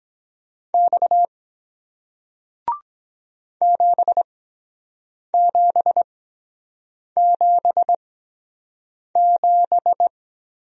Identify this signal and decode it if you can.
Morse code, heard as a keyed tone, about 23 wpm, spaced as for 7 wpm, 700 Hz. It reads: X 7777